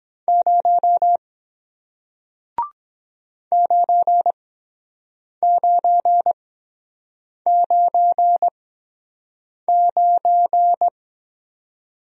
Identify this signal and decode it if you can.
Morse code, heard as a keyed tone, about 22 wpm, spaced as for 7 wpm, 700 Hz. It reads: 0 9999